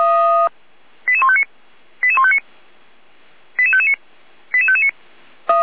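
Five-tone selective call (ZVEI), heard as quick runs of five tones, tones of about 70 ms. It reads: 88159 88159 88559 88559
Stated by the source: ZVEI five-tone selective call 88419, received by SDR.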